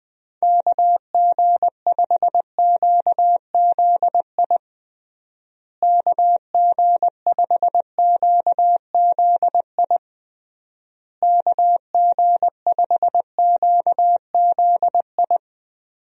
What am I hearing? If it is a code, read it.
Morse code, heard as a keyed tone, about 20 wpm, 700 Hz. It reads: KG5QZI KG5QZI KG5QZI
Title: USA Call Signs - 6 Characters - Repeated 3x - 20wpm